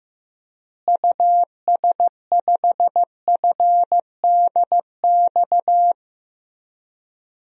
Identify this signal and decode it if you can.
Morse code, heard as a keyed tone, about 16 wpm, 700 Hz. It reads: US5FDX
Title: DXCC-100 Call Signs - 6 Characters - Repeated 2x - 15wpm